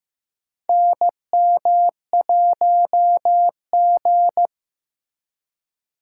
Morse code, heard as a keyed tone, about 15 wpm, 700 Hz. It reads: NM1G
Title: USA Call Signs - 4 Characters - 3x Character Spacing - 15wpm